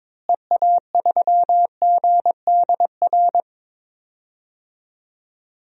Morse code, heard as a keyed tone, about 21 wpm, 700 Hz. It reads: EA3GDR
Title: DXCC-100 Call Signs - 6 Characters - Repeated 2x - 22wpm